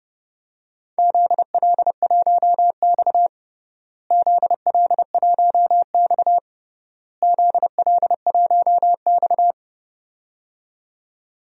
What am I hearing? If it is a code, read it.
Morse code, heard as a keyed tone, about 31 wpm, 700 Hz. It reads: ZL1X ZL1X ZL1X
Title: DXCC-100 Call Signs - 4 Characters - Repeated 3x - 30wpm